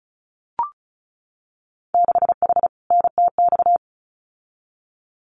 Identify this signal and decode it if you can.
Morse code, heard as a keyed tone, about 36 wpm, 700 Hz. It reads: 6H NTX
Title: Winter Field Day Exchanges Encore 35wpm